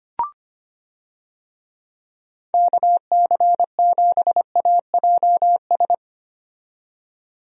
Morse code, heard as a keyed tone, about 25 wpm, 700 Hz. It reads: KC7AJS
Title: USA Call Signs - 6 Characters - 25wpm